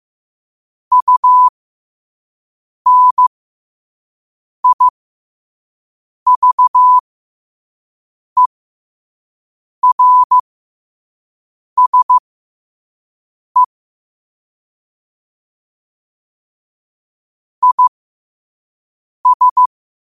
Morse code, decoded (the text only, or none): UNIVERSE IS